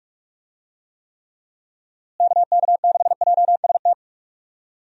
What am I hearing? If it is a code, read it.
Morse code, heard as a keyed tone, about 46 wpm, 700 Hz. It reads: KK6JST
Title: USA Call Signs - 6 Characters - Repeated 2x - 45wpm